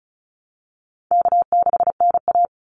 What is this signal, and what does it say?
Morse code, heard as a keyed tone, about 34 wpm, 700 Hz. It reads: K6NA
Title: Easy Call Signs Repeated 2x - 35wpm